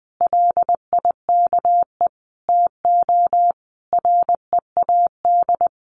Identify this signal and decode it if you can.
Morse code, heard as a keyed tone, about 19 wpm, 700 Hz. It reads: LIKE TO READ
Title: Sentences from Top 500 Words 20wpm